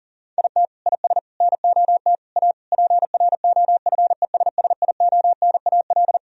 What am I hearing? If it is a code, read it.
Morse code, heard as a keyed tone, about 40 wpm, 700 Hz. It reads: IT IS NOT A PROFESSIONAL
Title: Sentences from Top 1000 Words - Review Part 4 - 40wpm